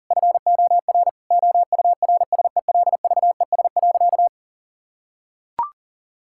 Morse code, heard as a keyed tone, about 40 wpm, 700 Hz. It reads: FOR OURSELVES.
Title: Sentences from Top 1600 Words 40wpm